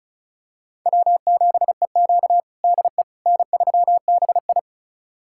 Morse code, heard as a keyed tone, about 34 wpm, 700 Hz. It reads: W7EQ DE N3BI
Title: USA Callsign DE Callsign - 4 Characters - 35wpm